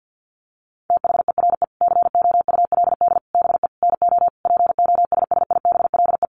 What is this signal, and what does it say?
Morse code, heard as a keyed tone, about 50 wpm, 700 Hz. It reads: THERE COULD BE NO POSSIBLE
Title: Sentences from Top 700 Words 50wpm